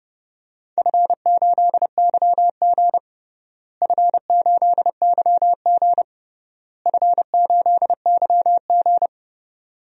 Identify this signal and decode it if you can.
Morse code, heard as a keyed tone, about 32 wpm, 700 Hz. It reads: F8YG F8YG F8YG
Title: DXCC-100 Call Signs - 4 Characters - Repeated 3x - 30wpm